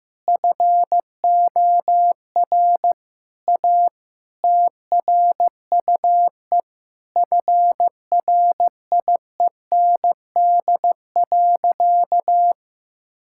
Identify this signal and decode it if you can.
Morse code, heard as a keyed tone, about 15 wpm, 700 Hz. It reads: FOR A TRUE FRIEND.
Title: Sentences from Top 1000 Words - Review Part 6 - 15wpm